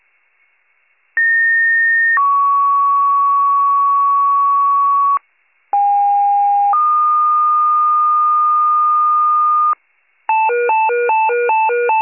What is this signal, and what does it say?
Two-tone page, heard as a one-second tone, then a three-second tone, about 1850 then 1120 Hz, about 810 then 1220 Hz. It is heard from 1.2 to 9.7 s.